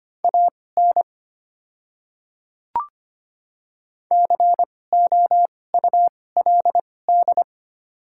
Morse code, heard as a keyed tone, about 25 wpm, spaced as for 18 wpm, 700 Hz. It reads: AN COULD
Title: 2x Character Spacing - Top 300 Words - 25wpm